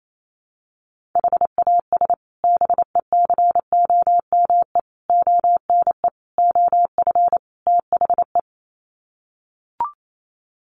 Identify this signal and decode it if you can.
Morse code, heard as a keyed tone, about 28 wpm, 700 Hz. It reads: HAS BECOME ONE OF THE
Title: Sets of 5 Words Encore 28wpm